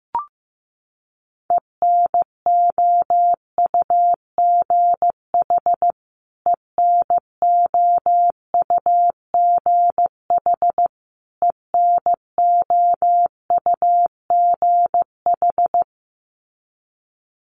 Morse code, heard as a keyed tone, about 15 wpm, 700 Hz. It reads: ENOUGH ENOUGH ENOUGH
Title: Top 500 Words Repeated 15wpm